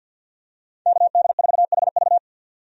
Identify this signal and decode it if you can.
Morse code, heard as a keyed tone, about 47 wpm, 700 Hz. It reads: KD4HV
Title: USA Call Signs - 5 Characters - 50wpm